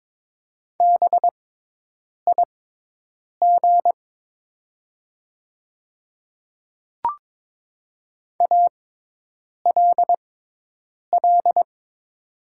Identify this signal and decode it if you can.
Morse code, heard as a keyed tone, about 22 wpm, spaced as for 8 wpm, 700 Hz. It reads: BIG ALL
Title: 6x Character Spacing - Top 200 Words - 22wpm